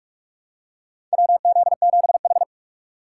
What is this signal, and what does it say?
Morse code, heard as a keyed tone, about 46 wpm, 700 Hz. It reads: WZ7H
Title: USA Call Signs - 4 Characters - 2x Character Spacing - 45wpm